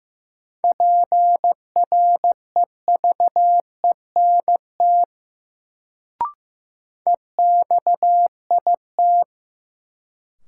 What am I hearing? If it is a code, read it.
Morse code, heard as a keyed tone, about 15 wpm, 700 Hz. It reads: PREVENT EXIT
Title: Sentences from Top 1100 Words - New Word Drill Alt - 15wpm